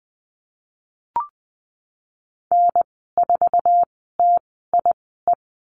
Morse code, heard as a keyed tone, about 20 wpm, spaced as for 15 wpm, 700 Hz. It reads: N4TIE
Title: 055 Lesson 10g - Letter r - Call Signs 20wpm